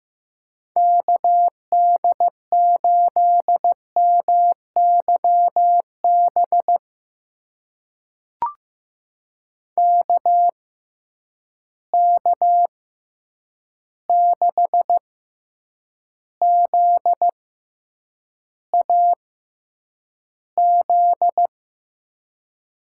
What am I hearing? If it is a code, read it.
Morse code, heard as a keyed tone, about 15 wpm, spaced as for 8 wpm, 700 Hz. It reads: KD8MYB KK6ZAZ
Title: USA Call Signs - 6x Character Spacing - 15wpm